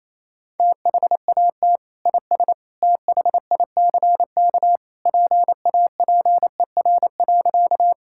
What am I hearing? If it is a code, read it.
Morse code, heard as a keyed tone, about 28 wpm, 700 Hz. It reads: THAT IS THICK PAPER.
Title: Sentences from Top 1600 Words 28wpm